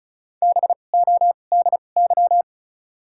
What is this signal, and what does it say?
Morse code, heard as a keyed tone, about 34 wpm, spaced as for 25 wpm, 700 Hz. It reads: BODY
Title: 2x Character Spacing - Top 500 Words - 35wpm